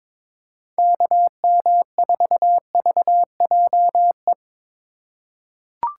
Morse code, heard as a keyed tone, about 21 wpm, 700 Hz. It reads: KM4VJE
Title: USA Call Signs - 4x Character Spacing - 22wpm